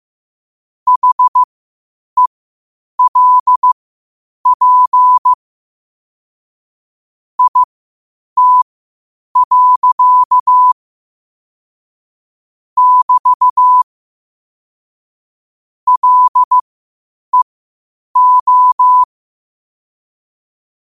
Morse code, heard as a keyed tone, about 15 wpm, spaced as for 8 wpm, 1000 Hz. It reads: HELP IT. <BT> LEO
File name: qotdfiles / 8wpm